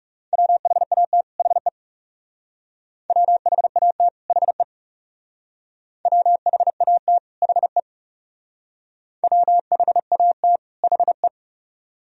Morse code, heard as a keyed tone, about 39 wpm, 700 Hz. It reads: WHAT HE WHAT HE WHAT HE WHAT HE